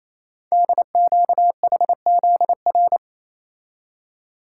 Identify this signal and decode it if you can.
Morse code, heard as a keyed tone, about 29 wpm, 700 Hz. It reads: DQHZR
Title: Code Groups - 5 Letter:Numbers - 28wpm